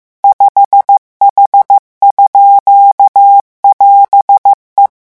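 Morse code, heard as a keyed tone, about 15 wpm, 800 Hz. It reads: 5H_<AS>E
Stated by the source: Morse revision lesson at 14 words per minute.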